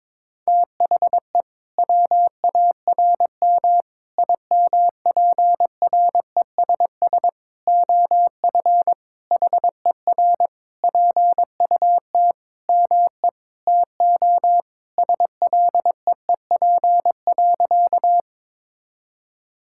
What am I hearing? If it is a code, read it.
Morse code, heard as a keyed tone, about 22 wpm, 700 Hz. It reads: THE WARM IMPRESS OF HER PUT ME TO SLEEP.
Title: Sentences from Top 1600 Words 22wpm